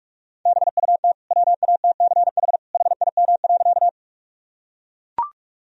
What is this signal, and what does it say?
Morse code, heard as a keyed tone, about 44 wpm, 700 Hz. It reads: BUT WATCH HIM.
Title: Sentences from Top 1000 Words - Review Part 1 - 45wpm